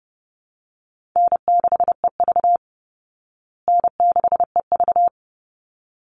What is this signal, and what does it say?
Morse code, heard as a keyed tone, about 30 wpm, 700 Hz. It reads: N6EV N6EV
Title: Easy Call Signs Repeated 2x - 30wpm